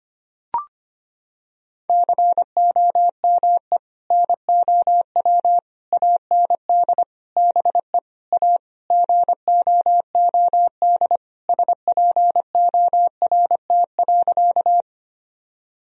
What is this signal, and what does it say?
Morse code, heard as a keyed tone, about 25 wpm, 700 Hz. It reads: COME NOW AND BE A GOOD SPORT.
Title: Sentences from Top 1000 Words - Review Part 4 - 25wpm